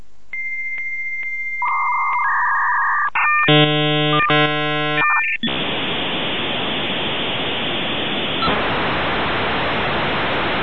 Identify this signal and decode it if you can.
Dial-up modem handshake.